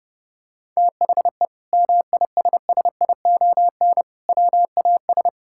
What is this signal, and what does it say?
Morse code, heard as a keyed tone, about 30 wpm, 700 Hz. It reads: THE MISSION WAS
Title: Sentences from Top 1600 Words 30wpm